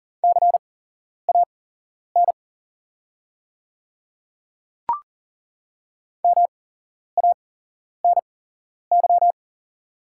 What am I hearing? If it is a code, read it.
Morse code, heard as a keyed tone, about 40 wpm, spaced as for 11 wpm, 700 Hz. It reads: CAN MANY
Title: 8x Character Spacing - Top 100 Words - 40wpm